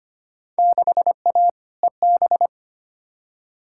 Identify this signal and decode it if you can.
Morse code, heard as a keyed tone, about 25 wpm, 700 Hz. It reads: 6A EB
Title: Field Day Exchanges Encore - 25wpm